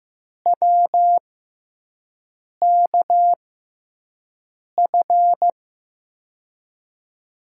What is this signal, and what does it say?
Morse code, heard as a keyed tone, about 15 wpm, spaced as for 5 wpm, 700 Hz. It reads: WKF
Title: USA Call Signs - 5 Characters - 6x Character Spacing - 15wpm